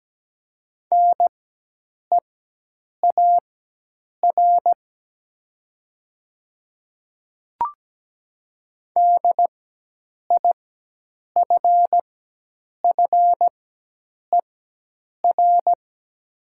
Morse code, heard as a keyed tone, about 17 wpm, spaced as for 8 wpm, 700 Hz. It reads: NEAR DIFFER
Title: 4x Character Spacing - Top 300 Words - 17wpm